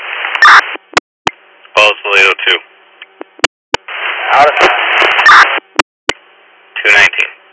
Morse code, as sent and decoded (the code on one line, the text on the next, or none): none